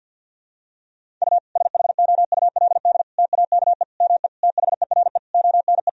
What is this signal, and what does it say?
Morse code, heard as a keyed tone, about 46 wpm, 700 Hz. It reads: U SHOULD TAKE ME THERE ONE
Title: Sentences from Top 300 Words and Abbreviations 50wpm